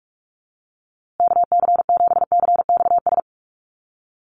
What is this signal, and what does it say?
Morse code, heard as a keyed tone, about 44 wpm, 700 Hz. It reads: KC7CXS
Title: USA Call Signs Repeated 3x 45wpm